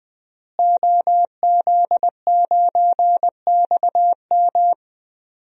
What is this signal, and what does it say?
Morse code, heard as a keyed tone, about 20 wpm, 700 Hz. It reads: OZ9XM
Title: DXCC-100 Call Signs - 5 Characters - Repeated 2x - 20wpm